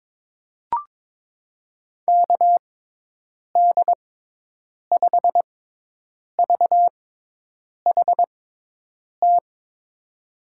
Morse code, heard as a keyed tone, about 21 wpm, spaced as for 8 wpm, 700 Hz. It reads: KD5VHT
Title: USA Call Signs - 6 Characters - 6x Character Spacing - 22wpm